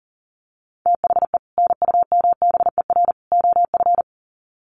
Morse code, heard as a keyed tone, about 40 wpm, 700 Hz. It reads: THE NUMBER OF
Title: Sets of 3 Words 40wpm